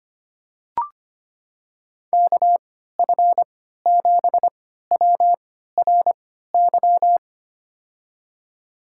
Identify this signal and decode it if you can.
Morse code, heard as a keyed tone, about 25 wpm, spaced as for 14 wpm, 700 Hz. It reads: KF7WRY
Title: USA Call Signs - 6 Characters - 3x Character Spacing - 25wpm